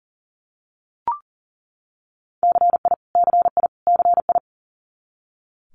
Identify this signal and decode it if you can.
Morse code, heard as a keyed tone, about 41 wpm, 700 Hz. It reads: CI CI CI
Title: Multi-letter Phonemes Repeated 3x 40wpm